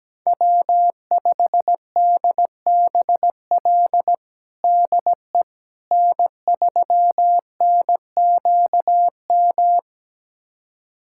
Morse code, heard as a keyed tone, about 17 wpm, 700 Hz. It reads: W5DBL DE N3NQM
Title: USA Callsign DE Callsign - 5 Characters - 17wpm